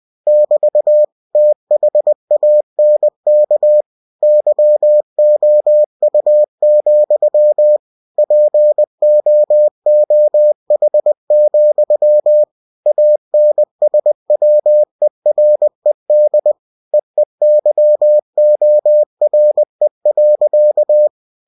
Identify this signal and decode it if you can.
Morse code, heard as a keyed tone, about 20 wpm, 600 Hz. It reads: <BT> THANK YOU, POOH, ANSWERED EEYORE.